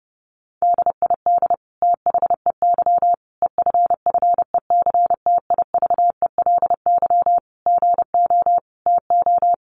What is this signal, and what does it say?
Morse code, heard as a keyed tone, about 28 wpm, 700 Hz. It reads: DID THEY EFFECTIVELY GO TO
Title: Sentences from Top 1400 Words 30wpm